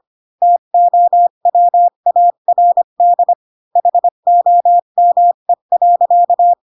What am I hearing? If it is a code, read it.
Morse code, heard as a keyed tone, about 25 wpm, 700 Hz. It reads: TOWARD HOME.
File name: Pinocchio / Pinocchio ch4 20 25 ik1hge